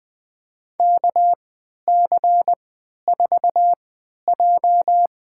Morse code, heard as a keyed tone, about 20 wpm, spaced as for 11 wpm, 700 Hz. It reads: KC4J
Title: USA Call Signs - 3x Character Spacing - 20wpm